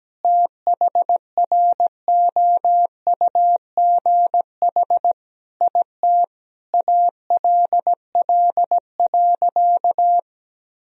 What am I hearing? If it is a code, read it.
Morse code, heard as a keyed tone, about 17 wpm, 700 Hz. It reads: THROUGH IT ALL.
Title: Sentences from Top 1000 Words - Review Part 1 - 17wpm